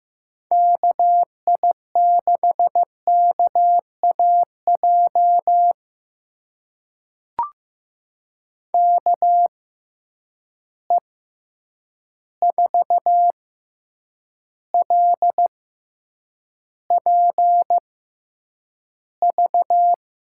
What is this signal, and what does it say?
Morse code, heard as a keyed tone, about 15 wpm, spaced as for 8 wpm, 700 Hz. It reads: KI6KAJ KE4LPV
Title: USA Call Signs - 6 Characters - 6x Character Spacing - 15wpm